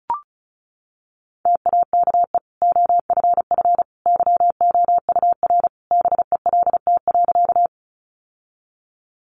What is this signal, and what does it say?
Morse code, heard as a keyed tone, about 34 wpm, 700 Hz. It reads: TAKE OFF YOUR BELT.
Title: Sentences from Top 1400 Words 35wpm